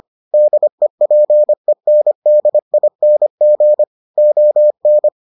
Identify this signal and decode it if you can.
Morse code, heard as a keyed tone, about 25 wpm, 600 Hz. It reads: DEPENDING ON